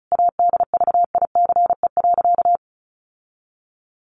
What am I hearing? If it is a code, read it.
Morse code, heard as a keyed tone, about 34 wpm, 700 Hz. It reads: ADVICE.
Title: Sentences from Top 1100 Words 35wpm